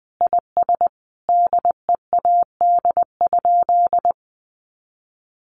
Morse code, heard as a keyed tone, about 20 wpm, 700 Hz. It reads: IS DEAD?
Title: Sentences from Top 1500 Words 20wpm